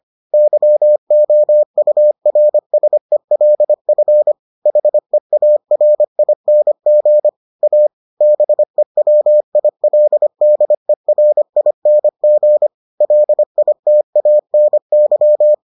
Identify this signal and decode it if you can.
Morse code, heard as a keyed tone, about 25 wpm, 600 Hz. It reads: YOURSELF HEARING A BEWILDERING LITANY